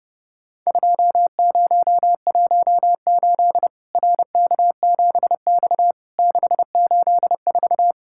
Morse code, heard as a keyed tone, about 30 wpm, 700 Hz. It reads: 2018 RK7X 684
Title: Worked All Europe DX Contest - QTC Exchange - 30wpm